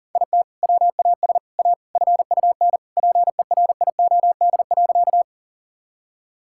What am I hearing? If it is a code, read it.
Morse code, heard as a keyed tone, about 37 wpm, 700 Hz. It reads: IT WAS A FUN PERIOD.
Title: Sentences from Top 1000 Words - Review Part 5 - 40wpm